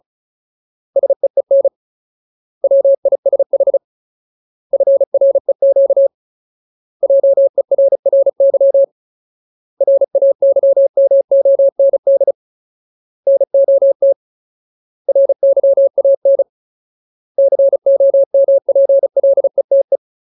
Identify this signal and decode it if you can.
Morse code, heard as a keyed tone, about 34 wpm, 550 Hz. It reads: SEEN WISH FREQ JERRY RAYMOND NOT RYAN COMPLETE
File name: Ultimate Word List / 35WPM Ultimate